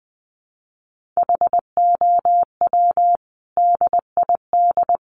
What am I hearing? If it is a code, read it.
Morse code, heard as a keyed tone, about 20 wpm, 700 Hz. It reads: HOW DID
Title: Sentences from Top 1400 Words 20wpm